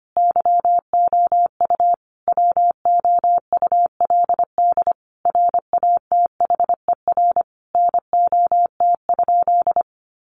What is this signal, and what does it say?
Morse code, heard as a keyed tone, about 25 wpm, 700 Hz. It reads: YOU WOULD RATHER NOT?
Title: Sentences from Top 700 Words 25wpm